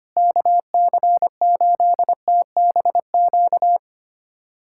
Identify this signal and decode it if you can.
Morse code, heard as a keyed tone, about 25 wpm, 700 Hz. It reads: KC8TBQ
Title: USA Callsign DE Callsign 25wpm